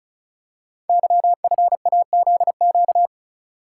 Morse code, heard as a keyed tone, about 36 wpm, 700 Hz. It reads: YFAZQ
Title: Code Groups - 5 Letter:Numbers - 35wpm